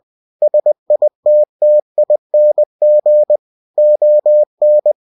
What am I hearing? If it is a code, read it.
Morse code, heard as a keyed tone, about 20 wpm, 600 Hz. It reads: SITTING ON
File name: cw-books / winnie-the-pooh / winnie-20wpm-page-0084